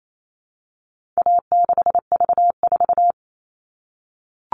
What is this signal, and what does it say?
Morse code, heard as a keyed tone, about 28 wpm, 700 Hz. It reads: A6V4
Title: All Cut Numbers - 4-Digit 28wpm